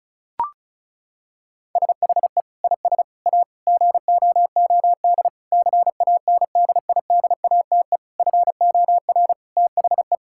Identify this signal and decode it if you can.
Morse code, heard as a keyed tone, about 34 wpm, 700 Hz. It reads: SHE IS A GOOD CANDIDATE FOR THE
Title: Sentences from Top 1600 Words 35wpm